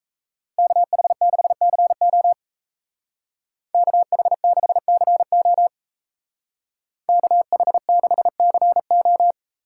Morse code, heard as a keyed tone, about 33 wpm, 700 Hz. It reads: KH6CO KH6CO KH6CO
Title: Speed Racing - USA 2x2 Call Signs 28wpm